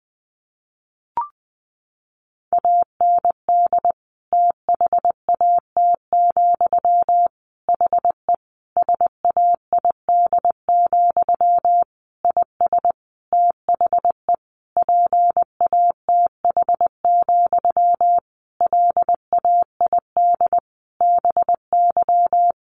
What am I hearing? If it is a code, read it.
Morse code, heard as a keyed tone, about 20 wpm, 700 Hz. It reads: AND THAT, HE SAID, IS THE PATH, LAID BY
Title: A Sound Of Thunder 20wpm